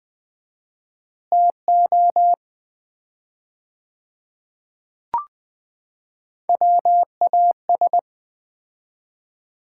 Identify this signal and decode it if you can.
Morse code, heard as a keyed tone, about 21 wpm, 700 Hz. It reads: TO WAS
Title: 4x Word-Spacing - Sets of 5 Words - 20wpm